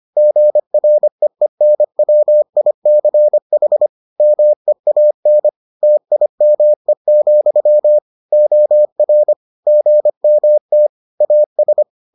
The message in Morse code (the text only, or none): GREENWICH MEAN TIME, OR GMT AS